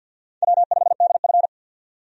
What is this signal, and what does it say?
Morse code, heard as a keyed tone, about 52 wpm, 700 Hz. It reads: W5DF
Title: USA Call Signs - 4 Characters - 50wpm